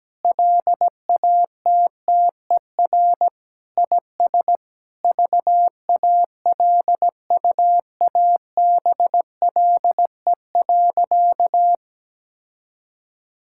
Morse code, heard as a keyed tone, about 17 wpm, 700 Hz. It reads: LATTER IS VALUABLE.
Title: Sentences from Top 1600 Words 17wpm